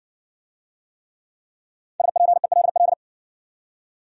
Morse code, heard as a keyed tone, about 60 wpm, 700 Hz. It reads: SPELL